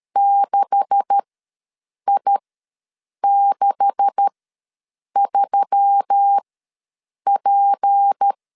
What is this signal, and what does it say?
Morse code, heard as a keyed tone, about 13 wpm, spaced as for 7 wpm, 800 Hz. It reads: 6I63P
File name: LEZIONI / 9 286 ws 12wpm 3351